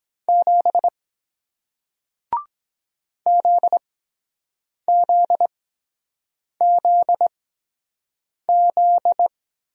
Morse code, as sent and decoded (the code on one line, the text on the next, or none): --... / --.. --.. --.. --..
7 ZZZZ